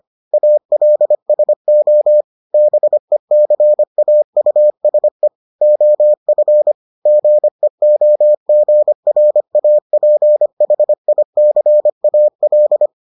Morse code, heard as a keyed tone, about 25 wpm, 600 Hz. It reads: ALSO BECAUSE OF GEOGRAPHICAL